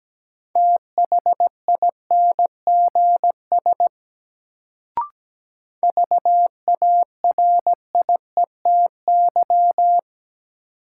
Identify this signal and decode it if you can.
Morse code, heard as a keyed tone, about 17 wpm, 700 Hz. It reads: THINGS VARIETY
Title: Sentences from Top 1000 Words - Review Part 4 - New Word Drill - 17wpm